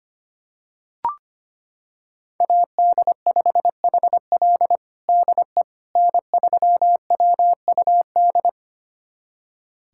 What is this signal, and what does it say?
Morse code, heard as a keyed tone, about 25 wpm, 700 Hz. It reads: AD5HL DE N3WUD
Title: USA Callsign DE Callsign - 5 Characters - 25wpm